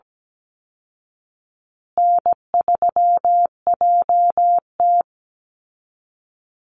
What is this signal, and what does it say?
Morse code, heard as a keyed tone, about 17 wpm, 700 Hz. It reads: N3JT
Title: Easy Call Signs Repeated 3x - 17wpm